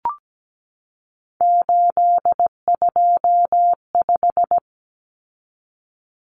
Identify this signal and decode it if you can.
Morse code, heard as a keyed tone, about 17 wpm, 700 Hz. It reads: 825
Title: Common Cut Numbers - 3-Digit 17wpm